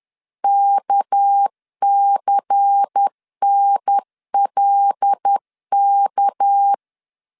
Morse code, heard as a keyed tone, about 10 wpm, 800 Hz. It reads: KCNLK
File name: LEZIONI / 6 PKX0 / 6 PK 10wpm 2436